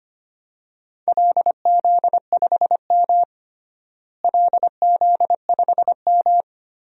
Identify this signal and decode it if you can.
Morse code, heard as a keyed tone, about 25 wpm, 700 Hz. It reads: LZ5M LZ5M
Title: DXCC-100 Call Signs - 4 Characters - Repeated 3x - 25wpm